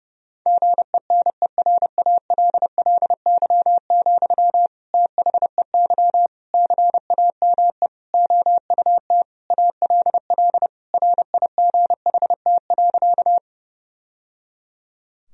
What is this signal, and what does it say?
Morse code, heard as a keyed tone, about 30 wpm, 700 Hz. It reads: GENERALLY, THEY CAME OUT ALL RIGHT.
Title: Sentences from Top 1000 Words - Review Part 4 - 30wpm